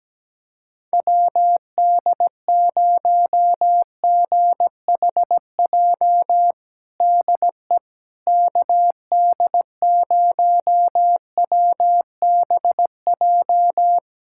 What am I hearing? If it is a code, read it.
Morse code, heard as a keyed tone, about 17 wpm, 700 Hz. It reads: WD0GHJ DE KD0WBJ
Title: USA Callsign DE Callsign - 6 Characters - 17wpm